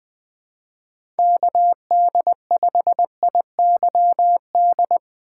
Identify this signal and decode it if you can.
Morse code, heard as a keyed tone, about 21 wpm, 700 Hz. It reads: KD5IYD
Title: USA Call Signs - 4x Character Spacing - 20wpm